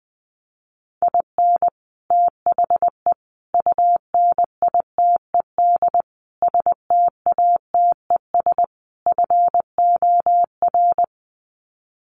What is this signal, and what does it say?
Morse code, heard as a keyed tone, about 20 wpm, 700 Hz. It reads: IN THE UNITED STATES FOR
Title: Sets of 5 Words Encore2 20wpm